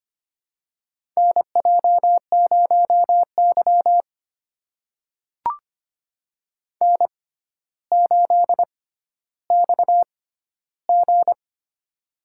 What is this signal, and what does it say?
Morse code, heard as a keyed tone, about 25 wpm, spaced as for 13 wpm, 700 Hz. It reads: NJ0Y N8XG
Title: USA Call Signs - 4 Characters - 6x Character Spacing - 25wpm